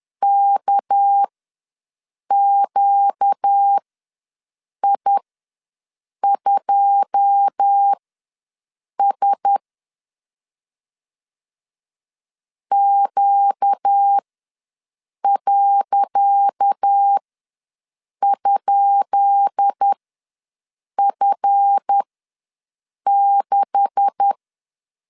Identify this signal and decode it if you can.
Morse code, heard as a keyed tone, about 10 wpm, spaced as for 6 wpm, 800 Hz. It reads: KQI2S Q.?F6
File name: LEZIONI / 10 SIGNS / 10C DOT ws 10wpm 4213